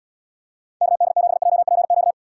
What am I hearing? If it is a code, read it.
Morse code, heard as a keyed tone, about 78 wpm, 700 Hz. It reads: KD7JPQ